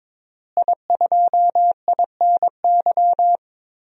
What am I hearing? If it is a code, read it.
Morse code, heard as a keyed tone, about 21 wpm, 700 Hz. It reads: I2INY
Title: DXCC-100 Call Signs - 5 Characters - Repeated 3x - 22wpm